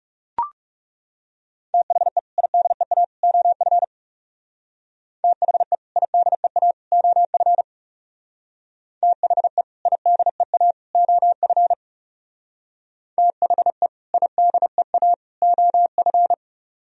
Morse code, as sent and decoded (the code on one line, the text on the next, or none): - .... / .. -.. .- / --- ..-. / - .... / .. -.. .- / --- ..-. / - .... / .. -.. .- / --- ..-. / - .... / .. -.. .- / --- ..-.
THE IDEA OF THE IDEA OF THE IDEA OF THE IDEA OF